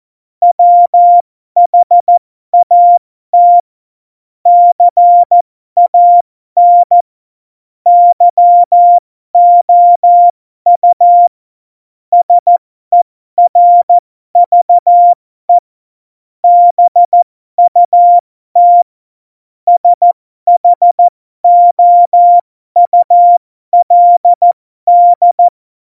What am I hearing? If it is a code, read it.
Morse code, heard as a keyed tone, about 14 wpm, spaced as for 12 wpm, 700 Hz. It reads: WHAT CAN YOU SERVE BUT SHOULD
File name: Morsekurs / Witze